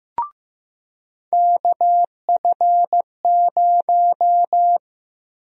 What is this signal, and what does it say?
Morse code, heard as a keyed tone, about 15 wpm, 700 Hz. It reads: KF0
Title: USA Call Sign Prefixes - Repeated 2x - 15wpm